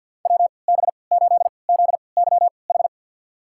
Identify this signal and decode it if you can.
Morse code, heard as a keyed tone, about 50 wpm, spaced as for 28 wpm, 700 Hz. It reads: WB8ZYH
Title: USA Call Signs - 6 Characters - 3x Character Spacing - 50wpm